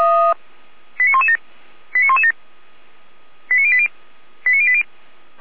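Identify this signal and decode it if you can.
Five-tone selective call (ZVEI), heard as quick runs of five tones, tones of about 70 ms. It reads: 89117 89117 89080 89080
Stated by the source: Received with an SDR; ZVEI five-tone selective call.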